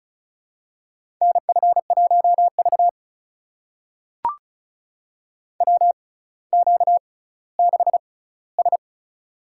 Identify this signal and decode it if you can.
Morse code, heard as a keyed tone, about 36 wpm, spaced as for 18 wpm, 700 Hz. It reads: NF1V WQ6S
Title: USA Call Signs - 4 Characters - 6x Character Spacing - 35wpm